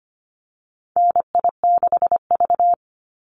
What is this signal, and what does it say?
Morse code, heard as a keyed tone, about 25 wpm, 700 Hz. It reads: NI6V
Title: Call Signs - 25wpm